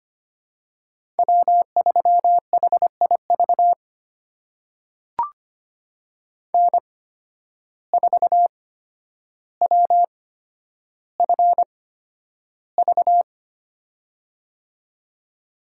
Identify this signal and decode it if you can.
Morse code, heard as a keyed tone, about 25 wpm, spaced as for 11 wpm, 700 Hz. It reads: W3HIV N4WFV
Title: USA Call Signs - 5 Characters - 8x Character Spacing - 25wpm